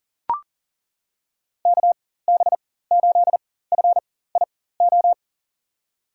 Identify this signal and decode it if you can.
Morse code, heard as a keyed tone, about 40 wpm, spaced as for 19 wpm, 700 Hz. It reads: KB8FIO
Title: USA Call Signs - 6 Characters - 4x Character Spacing - 40wpm